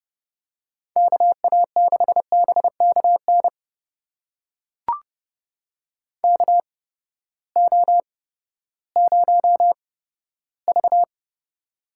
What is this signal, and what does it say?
Morse code, heard as a keyed tone, about 30 wpm, spaced as for 15 wpm, 700 Hz. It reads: KA6BKN KO0V
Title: USA Call Signs - 8x Character Spacing - 30wpm